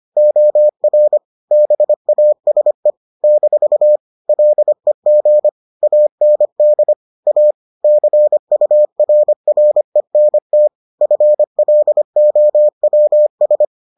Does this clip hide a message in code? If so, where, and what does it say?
Yes: Morse code from 0.2 to 13.6 s, heard as a keyed tone, about 25 wpm, 600 Hz. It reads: OR BASE - LEG AND A CURRENT FLOWS